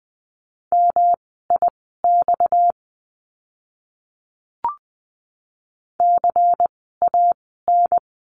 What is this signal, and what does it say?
Morse code, heard as a keyed tone, about 20 wpm, spaced as for 14 wpm, 700 Hz. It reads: MIX CAN
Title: 230 Lesson 37f - Letter x - Words 20wpm